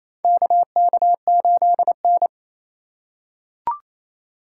Morse code, heard as a keyed tone, about 28 wpm, 700 Hz. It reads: KK8N
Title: USA Call Signs - 3x Character Spacing - 28wpm